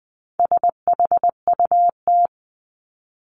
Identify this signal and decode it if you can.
Morse code, heard as a keyed tone, about 20 wpm, 700 Hz. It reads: SHUT
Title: Top 1751-2000 Words 20wpm